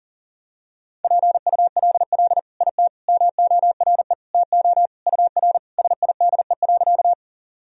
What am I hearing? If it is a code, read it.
Morse code, heard as a keyed tone, about 40 wpm, 700 Hz. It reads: PULL IT MORE TO UR SIDE.
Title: Sentences from Top 400 Words and Abbreviations 40wpm